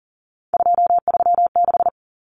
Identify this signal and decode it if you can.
Morse code, heard as a keyed tone, about 40 wpm, 700 Hz. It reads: 236
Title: Numbers - 3-Digit 40wpm